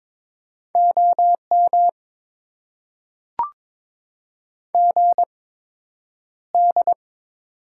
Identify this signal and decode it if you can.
Morse code, heard as a keyed tone, about 22 wpm, spaced as for 9 wpm, 700 Hz. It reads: OM GD
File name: Top 90 QSO Elements - 8x Character Spacing - 22wpm